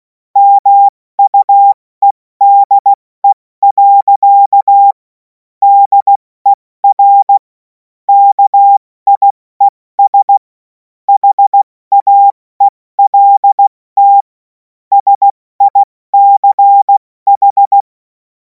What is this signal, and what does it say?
Morse code, heard as a keyed tone, about 16 wpm, 800 Hz. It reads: MUEDE. DER KIES HAELT SICH
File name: Morsekurs / Witze / Witz 10. Februar 2026 70 F80